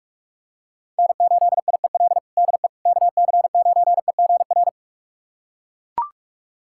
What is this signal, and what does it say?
Morse code, heard as a keyed tone, about 41 wpm, 700 Hz. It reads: N8IEL DE KC9EGR
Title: USA Callsign DE Callsign 45wpm